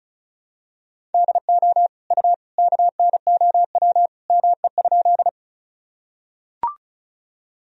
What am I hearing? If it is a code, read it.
Morse code, heard as a keyed tone, about 36 wpm, 700 Hz. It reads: DO U KNOW ME?
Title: Sentences from Top 200 Words and Abbreviations 35wpm